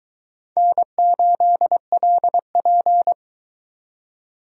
Morse code, heard as a keyed tone, about 24 wpm, 700 Hz. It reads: N8LP